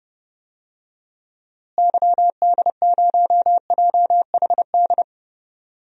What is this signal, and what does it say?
Morse code, heard as a keyed tone, about 30 wpm, 700 Hz. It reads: YD0JHD
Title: DXCC-100 Call Signs - 6 Characters - Repeated 2x - 30wpm